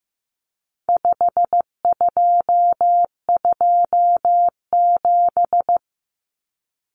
Morse code, heard as a keyed tone, about 15 wpm, 700 Hz. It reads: 5227
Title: Common Cut Numbers - 4-Digit 15wpm